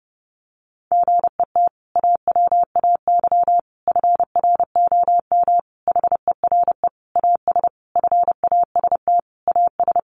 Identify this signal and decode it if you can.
Morse code, heard as a keyed tone, about 30 wpm, 700 Hz. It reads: GET AWAY FROM HERE AS FAST AS
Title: Sentences from Top 600 Words 30wpm